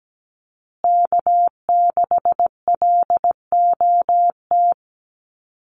Morse code, heard as a keyed tone, about 17 wpm, 700 Hz. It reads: K6LOT
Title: USA Call Signs Repeated 2x 17wpm